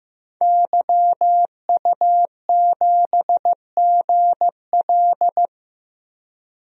Morse code, heard as a keyed tone, about 15 wpm, 700 Hz. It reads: YU7GL
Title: DXCC-100 Call Signs - 5 Characters - 15wpm